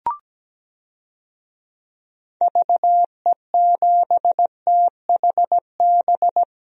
Morse code, heard as a keyed tone, about 17 wpm, 700 Hz. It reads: VE7THB
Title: DXCC-100 Call Signs - 6 Characters - Repeated 2x - 17wpm